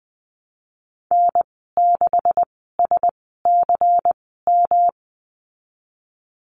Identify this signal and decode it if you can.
Morse code, heard as a keyed tone, about 19 wpm, spaced as for 14 wpm, 700 Hz. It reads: N6SCM
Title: 167 Lesson 27g - Number 7 - Call Signs 20wpm